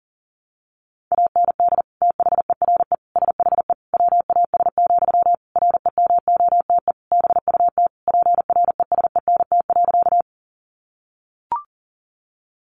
Morse code, heard as a keyed tone, about 40 wpm, 700 Hz. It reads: AND THERE SHE WAS, REMOTE BUT PRESENT.
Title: Sentences from Top 1300 Words 40wpm